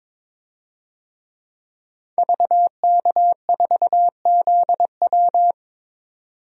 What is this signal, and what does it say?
Morse code, heard as a keyed tone, about 22 wpm, 700 Hz. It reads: VK4ZW
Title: DXCC-100 Call Signs - 5 Characters - 22wpm